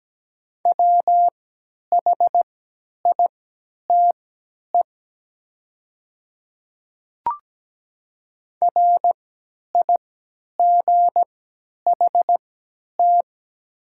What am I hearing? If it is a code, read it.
Morse code, heard as a keyed tone, about 17 wpm, spaced as for 10 wpm, 690 Hz. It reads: WHITE RIGHT
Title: 3x Character Spacing - Top 300 Words - 17wpm